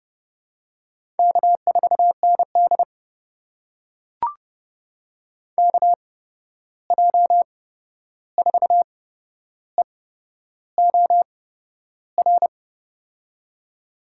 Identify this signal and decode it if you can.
Morse code, heard as a keyed tone, about 29 wpm, spaced as for 11 wpm, 700 Hz. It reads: K4ND KJ4EOR
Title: USA Call Signs - 8x Character Spacing - 30wpm